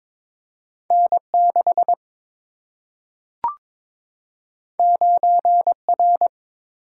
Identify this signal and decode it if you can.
Morse code, heard as a keyed tone, about 22 wpm, 700 Hz. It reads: N 6 9 R
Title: Single-Letter Single-Number 22wpm